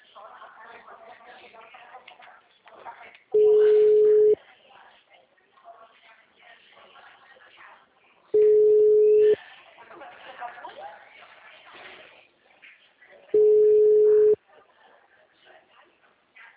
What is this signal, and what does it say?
Ringback tone (Europe, 425 Hz), with speech.